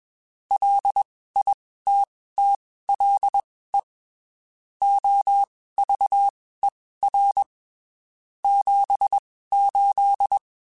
Morse code, heard as a keyed tone, about 21 wpm, spaced as for 15 wpm, 800 Hz. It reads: LITTLE OVER 78
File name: (spaced) / QSOs Spaced-mp3s / kc7sac @20 sp